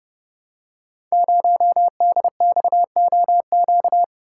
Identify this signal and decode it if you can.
Morse code, heard as a keyed tone, about 28 wpm, 700 Hz. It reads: 0DXOQ
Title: Code Groups - 5 Letter:Numbers - 30wpm